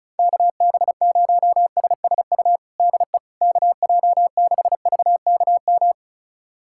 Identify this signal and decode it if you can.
Morse code, heard as a keyed tone, about 36 wpm, 700 Hz. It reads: KB0SSU DE KJ6VKM
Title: USA Callsign DE Callsign - 6 Characters - 35wpm